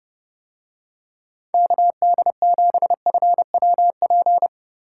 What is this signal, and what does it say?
Morse code, heard as a keyed tone, about 30 wpm, 700 Hz. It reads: KD7FWP